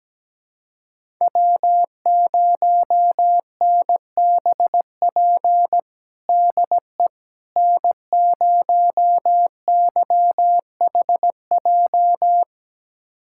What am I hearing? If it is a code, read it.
Morse code, heard as a keyed tone, about 17 wpm, 700 Hz. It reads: W0NBP DE N0YHJ
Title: USA Callsign DE Callsign - 5 Characters - 17wpm